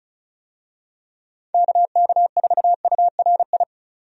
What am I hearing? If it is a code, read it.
Morse code, heard as a keyed tone, about 36 wpm, 700 Hz. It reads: KK4URI
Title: USA Call Signs - 6 Characters - 35wpm